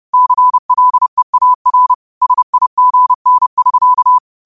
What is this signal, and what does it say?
Morse code, heard as a keyed tone, about 30 wpm, 1000 Hz. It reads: CLEAR SIGN<SK>
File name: qotdfiles / 30wpm